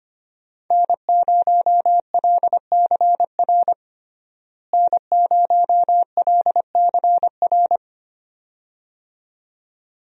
Morse code, heard as a keyed tone, about 25 wpm, 700 Hz. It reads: N0LCR N0LCR
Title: USA Call Signs - 5 Characters - Repeated 2x - 25wpm